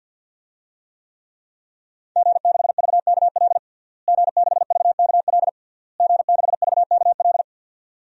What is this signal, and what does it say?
Morse code, heard as a keyed tone, about 50 wpm, 700 Hz. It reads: G6VKL G6VKL G6VKL
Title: DXCC-100 Call Signs - 5 Characters - Repeated 3x - 50wpm